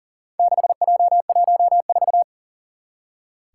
Morse code, heard as a keyed tone, about 40 wpm, 700 Hz. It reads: 6J14
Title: Four Challenging Letter-Number Combinations 40wpm